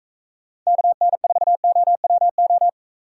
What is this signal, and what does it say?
Morse code, heard as a keyed tone, about 40 wpm, 700 Hz. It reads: KN4OWO